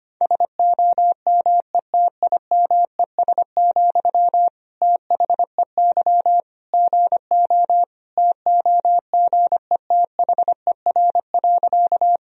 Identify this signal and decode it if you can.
Morse code, heard as a keyed tone, about 25 wpm, 700 Hz. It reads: SOMETIMES, THEY GO TOGETHER.
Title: Sentences from Top 1000 Words - Review Part 3 - 25wpm